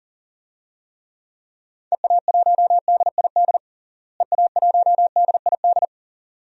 Morse code, heard as a keyed tone, about 40 wpm, 700 Hz. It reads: EA1DID EA1DID
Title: DXCC-100 Call Signs - 6 Characters - Repeated 2x - 40wpm